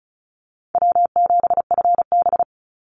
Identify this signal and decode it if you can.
Morse code, heard as a keyed tone, about 34 wpm, 700 Hz. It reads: W7FB
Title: Call Signs - 35wpm